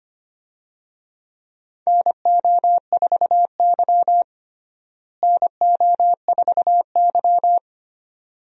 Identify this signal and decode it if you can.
Morse code, heard as a keyed tone, about 25 wpm, 700 Hz. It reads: NO4Y NO4Y
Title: USA Call Signs - 4 Characters - Repeated 2x - 25wpm